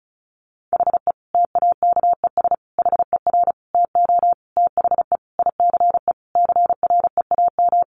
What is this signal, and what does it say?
Morse code, heard as a keyed tone, about 34 wpm, 700 Hz. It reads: HE TAKES HER TO THE ICE CREAM
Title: Additional Easy Phrases 35wpm